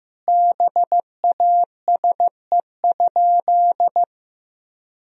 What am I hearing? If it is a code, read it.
Morse code, heard as a keyed tone, about 15 wpm, 700 Hz. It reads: BASE?
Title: Sentences from Top 1000 Words - Review Part 1 - 15wpm